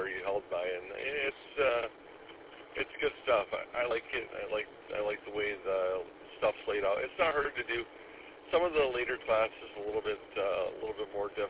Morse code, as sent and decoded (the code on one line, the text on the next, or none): none